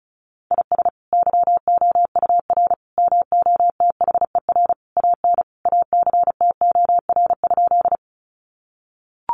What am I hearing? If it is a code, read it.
Morse code, heard as a keyed tone, about 34 wpm, 700 Hz. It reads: IS YOUR MOTHER AN ACTOR?
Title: Sentences from Top 1500 Words 35wpm